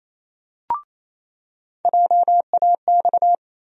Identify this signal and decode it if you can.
Morse code, heard as a keyed tone, about 28 wpm, 700 Hz. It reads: JAX
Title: USA Call Sign Postfixes - 3 Letters 28wpm